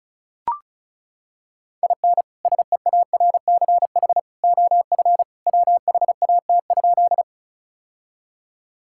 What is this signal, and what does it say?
Morse code, heard as a keyed tone, about 34 wpm, 700 Hz. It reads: IN SEARCH OF WHAT?
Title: Sentences from Top 1000 Words - Review Part 5 - 35wpm